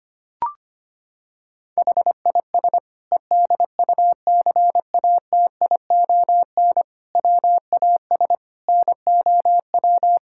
Morse code, heard as a keyed tone, about 25 wpm, 700 Hz. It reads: HIS EDUCATION WAS NOW
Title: Sentences from Top 1000 Words - Review Part 6 - 25wpm